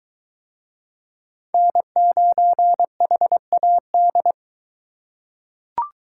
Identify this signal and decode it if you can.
Morse code, heard as a keyed tone, about 23 wpm, 700 Hz. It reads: N9HAD